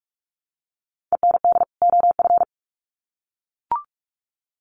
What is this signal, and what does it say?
Morse code, heard as a keyed tone, about 44 wpm, 700 Hz. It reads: END OF
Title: Sets of 2 Words 45wpm